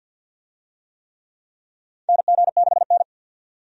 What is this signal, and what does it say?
Morse code, heard as a keyed tone, about 52 wpm, 700 Hz. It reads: NG6N
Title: USA Call Signs - 4 Characters - 50wpm